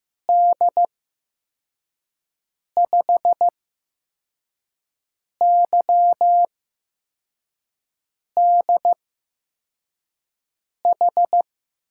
Morse code, heard as a keyed tone, about 15 wpm, spaced as for 4 wpm, 700 Hz. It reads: D5YDH
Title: USA Call Signs - 8x Character Spacing - 15wpm